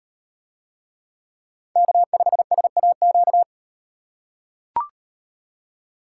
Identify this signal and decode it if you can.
Morse code, heard as a keyed tone, about 36 wpm, 700 Hz. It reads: K5SAQ